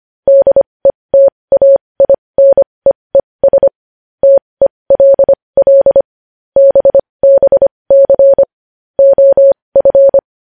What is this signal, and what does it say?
Morse code, heard as a keyed tone, about 25 wpm, spaced as for 20 wpm, 550 Hz. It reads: DETAINEES TELL BBC OF
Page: news headlines in morse code at 20 wpm